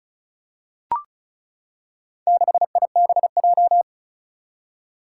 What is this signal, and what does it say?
Morse code, heard as a keyed tone, about 36 wpm, 700 Hz. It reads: 6IBJ
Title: Four Challenging Letter-Number Combinations 35wpm